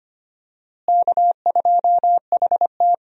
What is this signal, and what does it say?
Morse code, heard as a keyed tone, about 25 wpm, 700 Hz. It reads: K2HT
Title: USA Call Signs - 4 Characters - Repeated 3x - 25wpm